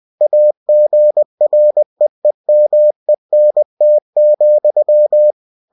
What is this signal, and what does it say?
Morse code, heard as a keyed tone, about 20 wpm, 600 Hz. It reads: AGREEMENT,